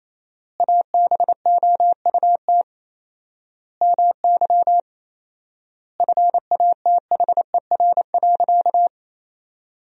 Morse code, heard as a keyed tone, about 28 wpm, 700 Hz. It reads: ABOUT MY FATHER.
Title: 4x Word-Spacing - Sentences from Top 400 Words - 28wpm